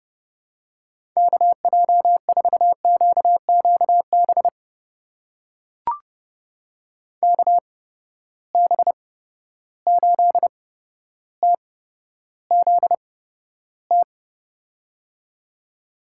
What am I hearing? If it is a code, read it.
Morse code, heard as a keyed tone, about 30 wpm, spaced as for 13 wpm, 700 Hz. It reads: KJ4QQB KB8TZT
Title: USA Call Signs - 6 Characters - 8x Character Spacing - 30wpm